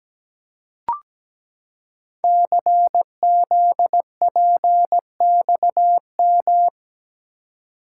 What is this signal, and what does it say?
Morse code, heard as a keyed tone, about 17 wpm, 700 Hz. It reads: CZPXM
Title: Code Groups - 5 Letters - 17wpm